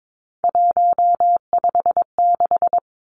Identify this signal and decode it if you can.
Morse code, heard as a keyed tone, about 22 wpm, 700 Hz. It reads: 156
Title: Three Challenging Letter-Number Combinations 22wpm